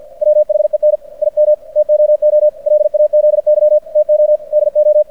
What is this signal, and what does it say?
Morse code, heard as a keyed tone, about 50 wpm, 600 Hz. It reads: WHEN IM TOO LAZY TO DO
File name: archived / CRSnet